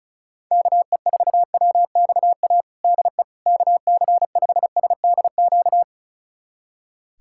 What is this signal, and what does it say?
Morse code, heard as a keyed tone, about 34 wpm, 700 Hz. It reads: KE4WXA DE KC5SDQ
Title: USA Callsign DE Callsign - 6 Characters - 35wpm